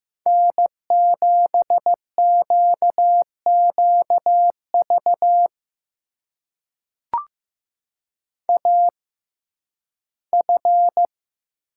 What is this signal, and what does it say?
Morse code, heard as a keyed tone, about 15 wpm, spaced as for 11 wpm, 700 Hz. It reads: N7QQV AF
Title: USA Call Signs - 5 Characters - 6x Character Spacing - 15wpm